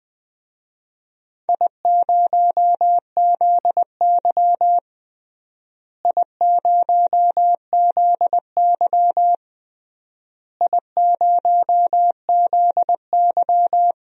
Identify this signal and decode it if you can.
Morse code, heard as a keyed tone, about 21 wpm, 700 Hz. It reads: I0ZY I0ZY I0ZY
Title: DXCC-100 Call Signs - 4 Characters - Repeated 3x - 20wpm